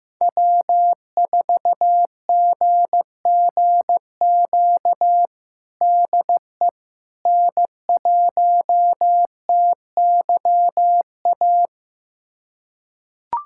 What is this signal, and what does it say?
Morse code, heard as a keyed tone, about 16 wpm, 700 Hz. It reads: W4GGQ DE N1TYA
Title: USA Callsign DE Callsign - 5 Characters - 15wpm